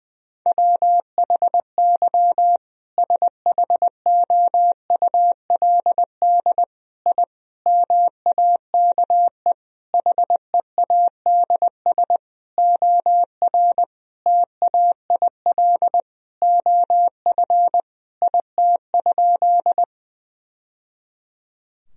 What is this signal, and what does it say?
Morse code, heard as a keyed tone, about 20 wpm, 700 Hz. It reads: WHY SHOULD I MAKE HEADS OR TAIL OF IT?
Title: Sentences from Top 1000 Words - Review Part 3 - 20wpm